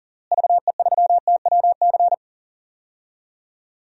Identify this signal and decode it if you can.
Morse code, heard as a keyed tone, about 40 wpm, 700 Hz. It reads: VE3TWC
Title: DXCC-100 Call Signs - 6 Characters - Repeated 3x - 40wpm